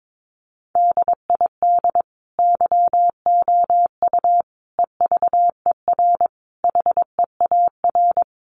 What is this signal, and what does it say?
Morse code, heard as a keyed tone, about 22 wpm, 700 Hz. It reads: DID YOU EVER HEAR
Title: Sentences from Top 400 Words 22wpm